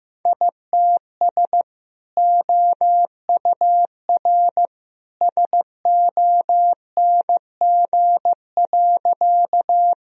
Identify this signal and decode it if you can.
Morse code, heard as a keyed tone, about 15 wpm, 700 Hz. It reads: ITS OUR SONG.
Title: Sentences from Top 1000 Words - Review Part 2 - 15wpm